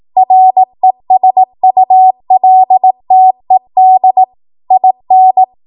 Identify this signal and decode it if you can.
Morse code, heard as a keyed tone, about 18 wpm, 750 Hz. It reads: RESULTED IN
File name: Morse / 18WPM